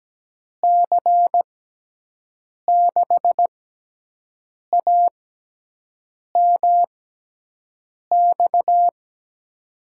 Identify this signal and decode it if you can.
Morse code, heard as a keyed tone, about 17 wpm, spaced as for 6 wpm, 700 Hz. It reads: C6AMX
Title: USA Call Signs - 6x Character Spacing - 17wpm